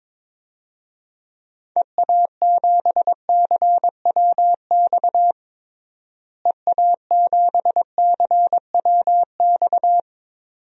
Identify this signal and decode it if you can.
Morse code, heard as a keyed tone, about 22 wpm, 700 Hz. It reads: EA7CWX EA7CWX
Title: DXCC-100 Call Signs - 6 Characters - Repeated 2x - 22wpm